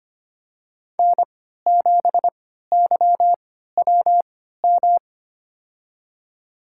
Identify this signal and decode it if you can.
Morse code, heard as a keyed tone, about 25 wpm, spaced as for 14 wpm, 700 Hz. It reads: N7YWM
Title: USA Call Signs - 3x Character Spacing - 25wpm